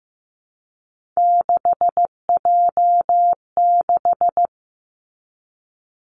Morse code, heard as a keyed tone, about 15 wpm, 700 Hz. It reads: 6J6
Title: Three Challenging Letter-Number Combinations 15wpm